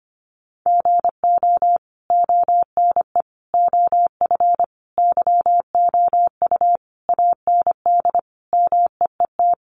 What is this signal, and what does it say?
Morse code, heard as a keyed tone, about 25 wpm, 700 Hz. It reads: GO ONE OF YOU AND MEET
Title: Sentences from Top 1100 Words 25wpm